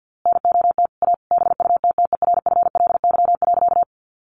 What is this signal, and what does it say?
Morse code, heard as a keyed tone, about 50 wpm, 700 Hz. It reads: NOT A BUTTERFLY.
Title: A Sound Of Thunder 50wpm